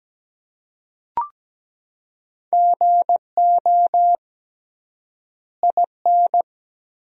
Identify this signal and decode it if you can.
Morse code, heard as a keyed tone, about 17 wpm, 700 Hz. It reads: GO IN
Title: 3x Word-Spacing - Sentences from Top 500 Words - 17wpm